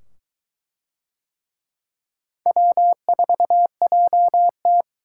Morse code, heard as a keyed tone, about 22 wpm, 700 Hz. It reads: W4JT